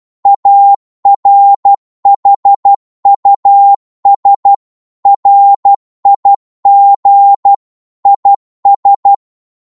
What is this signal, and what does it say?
Morse code, heard as a keyed tone, about 12 wpm, 800 Hz. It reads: ARHUS RIG IS
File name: Morse / qso-12wpm